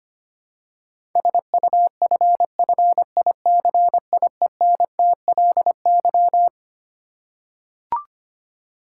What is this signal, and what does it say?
Morse code, heard as a keyed tone, about 25 wpm, 700 Hz. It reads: SUFFICIENTLY
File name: Twelve Letter Words - 25wpm